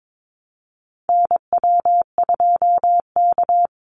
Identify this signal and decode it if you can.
Morse code, heard as a keyed tone, about 22 wpm, 690 Hz. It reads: NW2K
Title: Call Signs Repeated 3x - 22wpm